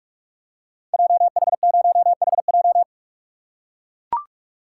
Morse code, heard as a keyed tone, about 44 wpm, 700 Hz. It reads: JH0HJ
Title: Code Groups - 5 Letter:Numbers - 45wpm